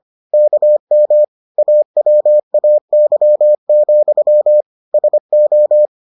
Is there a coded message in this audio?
Yes: Morse code from 0.3 to 5.9 s, heard as a keyed tone, about 25 wpm, 600 Hz. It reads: KM AWAY, SO